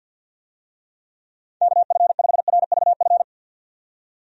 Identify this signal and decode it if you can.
Morse code, heard as a keyed tone, about 51 wpm, 700 Hz. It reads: KF5RVF
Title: USA Call Signs - 6 Characters - 50wpm